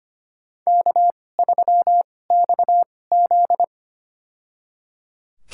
Morse code, heard as a keyed tone, about 25 wpm, spaced as for 18 wpm, 700 Hz. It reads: K3XZ
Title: USA Call Signs - 4 Characters - 2x Character Spacing - 25wpm